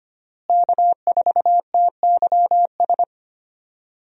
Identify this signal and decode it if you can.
Morse code, heard as a keyed tone, about 25 wpm, 700 Hz. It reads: K4TYS
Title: USA Call Signs - 5 Characters - Repeated 2x - 25wpm